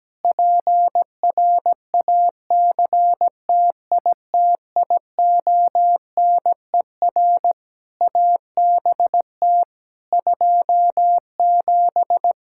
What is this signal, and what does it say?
Morse code, heard as a keyed tone, about 17 wpm, 700 Hz. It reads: PRACTITIONER ABT 27
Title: Practice QSO - Exchange 3 - More Info - 17wpm